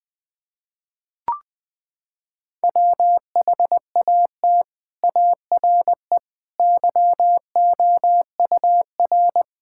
Morse code, heard as a keyed tone, about 21 wpm, 700 Hz. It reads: WHAT ARE YOUR
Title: Sentences with 6-Word Phrases - 20wpm